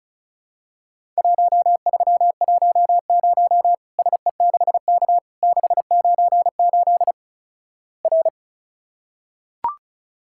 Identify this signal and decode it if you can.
Morse code, heard as a keyed tone, about 36 wpm, 700 Hz. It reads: 1310 SE6K 698 R
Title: Worked All Europe DX Contest - QTC Exchange - 35wpm